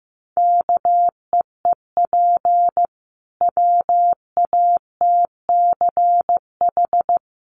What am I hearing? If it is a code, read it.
Morse code, heard as a keyed tone, about 14 wpm, 700 Hz. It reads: KEEP WATCH
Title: Sentences from Top 300 Words 15wpm